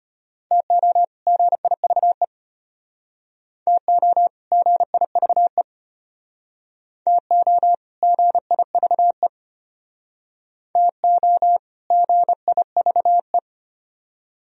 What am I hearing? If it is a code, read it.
Morse code, heard as a keyed tone, about 31 wpm, 700 Hz. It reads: TO GIVE TO GIVE TO GIVE TO GIVE
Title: Speed-Racing - Sets of 2 Words 25wpm